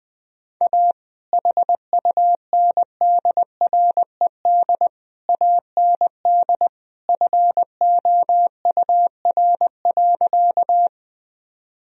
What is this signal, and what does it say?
Morse code, heard as a keyed tone, about 21 wpm, 700 Hz. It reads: A HUNDRED AND FOUR.
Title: Sentences from Top 1000 Words - Review Part 2 - 20wpm